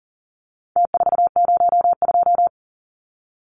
Morse code, heard as a keyed tone, about 40 wpm, 700 Hz. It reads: T402
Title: Numbers - Contest Style - 4-Digit 40wpm